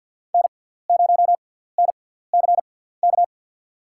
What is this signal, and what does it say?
Morse code, heard as a keyed tone, about 52 wpm, spaced as for 17 wpm, 700 Hz. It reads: N0NCK
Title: USA Call Signs - 5 Characters - 6x Character Spacing - 50wpm